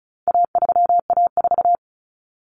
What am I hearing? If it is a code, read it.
Morse code, heard as a keyed tone, about 34 wpm, 700 Hz. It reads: A3A4
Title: Common Cut Numbers - 4-Digit 35wpm